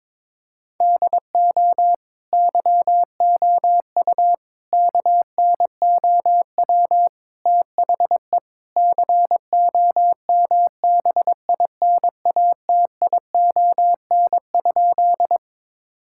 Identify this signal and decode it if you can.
Morse code, heard as a keyed tone, about 22 wpm, 700 Hz. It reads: DO YOU KNOW THE COMBINATION?
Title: Sentences from Top 1000 Words - Review Part 6 - 22wpm